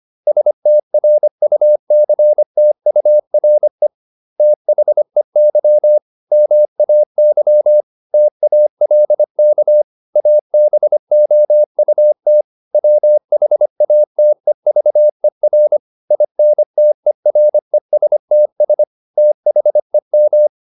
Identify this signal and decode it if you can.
Morse code, heard as a keyed tone, about 25 wpm, 600 Hz. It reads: STRUCTURE THEY MAY TALK ABOUT WHATEVER INTERESTS THEM